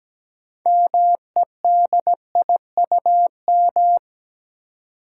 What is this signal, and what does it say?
Morse code, heard as a keyed tone, about 17 wpm, 700 Hz. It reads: MEDIUM
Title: Six Letter Words - 17wpm